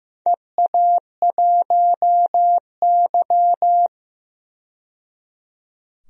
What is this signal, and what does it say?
Morse code, heard as a keyed tone, about 15 wpm, 700 Hz. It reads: EA1Y
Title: DXCC-100 Call Signs - 4 Characters - 15wpm